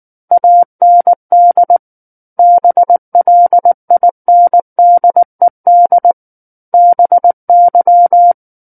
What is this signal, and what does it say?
Morse code, heard as a keyed tone, about 19 wpm, 700 Hz. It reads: AND BLINDED BY
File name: Book One / 10-Chapter 10 - 19 WPM